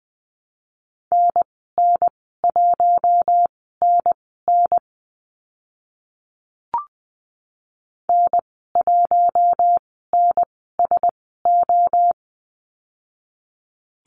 Morse code, heard as a keyed tone, about 20 wpm, spaced as for 14 wpm, 700 Hz. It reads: NN1NN N1NSO
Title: 042 Lesson 8g - Number 1 - Call Signs 20wpm